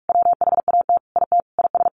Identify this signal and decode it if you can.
Morse code, heard as a keyed tone, about 44 wpm, 700 Hz. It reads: WHAT IT IS
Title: Sentences from Top 1500 Words 45wpm